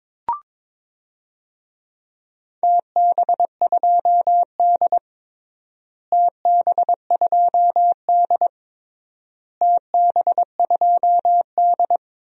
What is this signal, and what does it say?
Morse code, heard as a keyed tone, about 21 wpm, 700 Hz. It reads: TB2D TB2D TB2D
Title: DXCC-100 Call Signs - 4 Characters - Repeated 3x - 22wpm